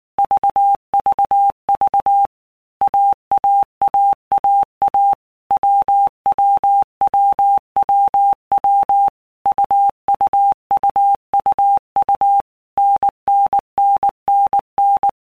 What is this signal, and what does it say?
Morse code, heard as a keyed tone, about 19 wpm, 800 Hz. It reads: VVV AAAAA WWWWW UUUUU NNNNN
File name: Basic Letter Groups By Blocks / Groups@18 / AWUNDG@18